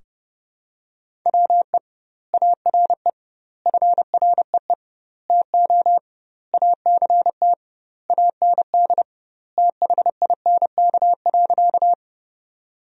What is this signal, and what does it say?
Morse code, heard as a keyed tone, about 30 wpm, 700 Hz. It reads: WE ARE FREE TO ACT AND THINK.
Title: 2x Word-Spacing - Sentences from Top 500 Words - 30wpm